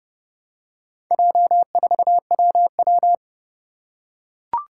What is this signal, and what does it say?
Morse code, heard as a keyed tone, about 30 wpm, 700 Hz. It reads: J4WW
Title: Four Challenging Letter-Number Combinations 30wpm